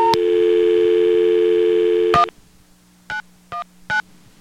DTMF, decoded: 1 919